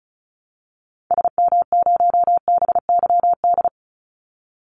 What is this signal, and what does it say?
Morse code, heard as a keyed tone, about 34 wpm, 700 Hz. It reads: SM0BYD
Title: Call Signs Repeated 2x - 35wpm